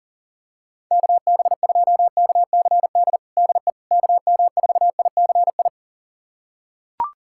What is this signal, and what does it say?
Morse code, heard as a keyed tone, about 40 wpm, 700 Hz. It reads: KB2KCD DE KM4ICI
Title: USA Callsign DE Callsign 40wpm